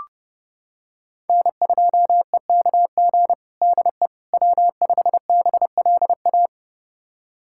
Morse code, heard as a keyed tone, about 30 wpm, 700 Hz. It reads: N2EKG DE W5BLA
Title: USA Callsign DE Callsign - 5 Characters - 30wpm